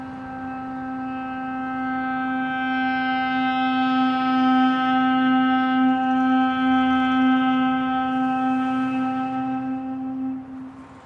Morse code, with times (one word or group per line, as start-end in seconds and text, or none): none